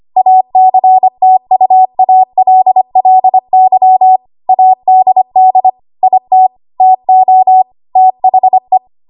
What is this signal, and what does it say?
Morse code, heard as a keyed tone, about 25 wpm, 750 Hz. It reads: ACTUALLY ADD IT TO THE
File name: Archive / 25 WPM / 25WPM